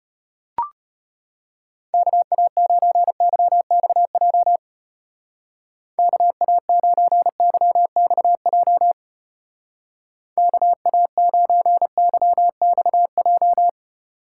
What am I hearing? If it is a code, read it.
Morse code, heard as a keyed tone, about 30 wpm, 700 Hz. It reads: KA9YXJ KA9YXJ KA9YXJ
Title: Speed Racing - USA 2x3 Call Signs 25wpm